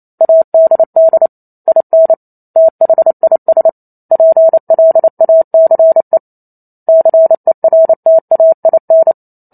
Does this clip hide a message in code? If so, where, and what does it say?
Yes: Morse code from 0.2 to 9.1 s, heard as a keyed tone, about 28 wpm, 650 Hz. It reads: ADD IN THIS PLACE CERTAIN